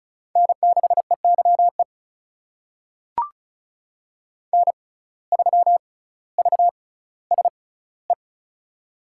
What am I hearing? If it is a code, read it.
Morse code, heard as a keyed tone, about 36 wpm, spaced as for 18 wpm, 700 Hz. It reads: N6EYE N3VSE
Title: USA Call Signs - 5 Characters - 6x Character Spacing - 35wpm